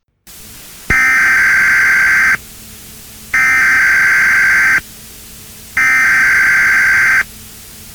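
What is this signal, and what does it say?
Emergency Alert System (SAME) tones (header bursts), heard as harsh buzzing bursts. It is heard from 0.9 to 7.2 s.
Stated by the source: Emergency Alert System tone used as a sound in a TF2 (Team Fortress 2) custom map.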